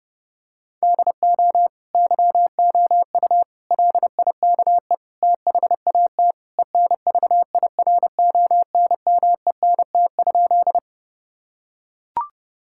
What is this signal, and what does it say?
Morse code, heard as a keyed tone, about 30 wpm, 700 Hz. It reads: DO YOU LIKE THAT ENVIRONMENT?
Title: Sentences from Top 1000 Words - Review Part 5 - 30wpm